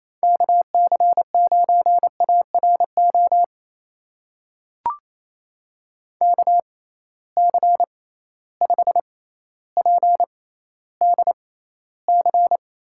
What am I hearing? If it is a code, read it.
Morse code, heard as a keyed tone, about 27 wpm, spaced as for 14 wpm, 700 Hz. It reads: KC9ARO KC5PDC
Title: USA Call Signs - 6 Characters - 6x Character Spacing - 28wpm